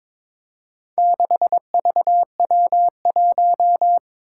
Morse code, heard as a keyed tone, about 22 wpm, 700 Hz. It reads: 6VW1
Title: Four Challenging Letter-Number Combinations 22wpm